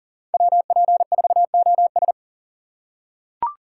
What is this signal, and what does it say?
Morse code, heard as a keyed tone, about 40 wpm, 700 Hz. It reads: WP4OS
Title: USA Call Signs - 5 Characters - 2x Character Spacing - 40wpm